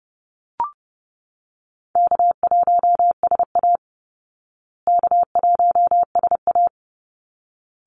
Morse code, heard as a keyed tone, about 30 wpm, 700 Hz. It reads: K1SA K1SA
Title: Easy Call Signs Repeated 2x - 30wpm